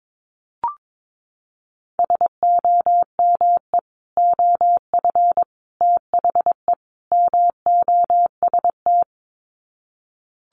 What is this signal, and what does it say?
Morse code, heard as a keyed tone, about 22 wpm, 700 Hz. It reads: SOME OF THE MOST
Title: Sets of 4 Words 22wpm